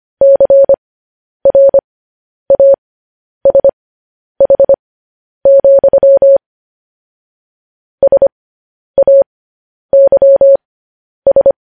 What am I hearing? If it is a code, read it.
Morse code, heard as a keyed tone, about 25 wpm, spaced as for 10 wpm, 550 Hz. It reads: CRASH, SAYS
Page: News Headlines in Morse Code at 10 WPM